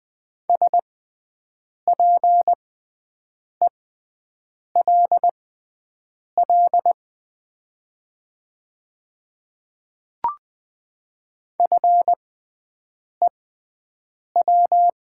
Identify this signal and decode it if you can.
Morse code, heard as a keyed tone, about 20 wpm, spaced as for 7 wpm, 700 Hz. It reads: SPELL FEW
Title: 6x Character Spacing - Top 300 Words - 20wpm